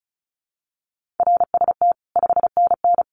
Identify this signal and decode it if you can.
Morse code, heard as a keyed tone, about 34 wpm, 700 Hz. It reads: RST 5NN
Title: Common QSO Phrases 35wpm